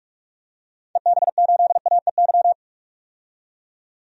Morse code, heard as a keyed tone, about 46 wpm, 700 Hz. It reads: EB8AEY
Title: DXCC-100 Call Signs - 6 Characters - 45wpm